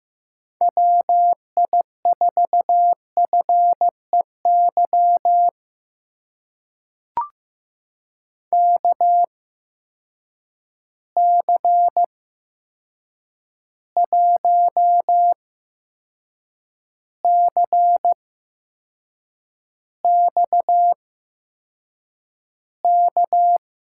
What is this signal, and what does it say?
Morse code, heard as a keyed tone, about 15 wpm, spaced as for 6 wpm, 700 Hz. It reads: WI4FEY KC1CXK
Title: USA Call Signs - 6 Characters - 8x Character Spacing - 15wpm